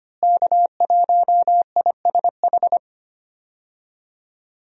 Morse code, heard as a keyed tone, about 25 wpm, 700 Hz. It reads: K1ISH
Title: USA Call Signs - 5 Characters - Repeated 3x - 25wpm